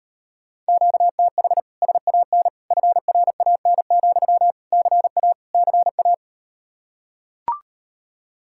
Morse code, heard as a keyed tone, about 39 wpm, 700 Hz. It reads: QTH SAN FRAN, CA CA